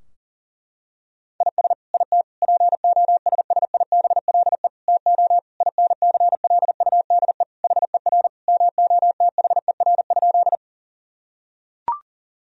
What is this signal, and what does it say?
Morse code, heard as a keyed tone, about 37 wpm, 700 Hz. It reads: IS IT POSSIBLE TO INCLUDE HER MOTHER?
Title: Sentences from Top 1000 Words - Review Part 3 - 40wpm